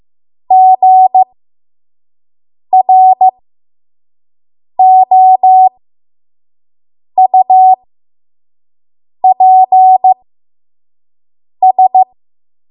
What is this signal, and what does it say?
Morse code, heard as a keyed tone, about 15 wpm, spaced as for 5 wpm, 750 Hz. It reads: GROUPS
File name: Morse / 05WPM